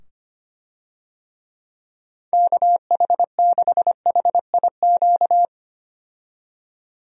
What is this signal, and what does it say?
Morse code, heard as a keyed tone, about 25 wpm, 700 Hz. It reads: KH6HIQ
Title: USA Call Signs - 6 Characters - 25wpm